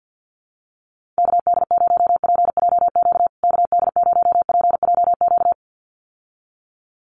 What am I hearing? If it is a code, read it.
Morse code, heard as a keyed tone, about 50 wpm, 700 Hz. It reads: KD0PJQ KD0PJQ
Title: USA Call Signs Repeated 2x 50wpm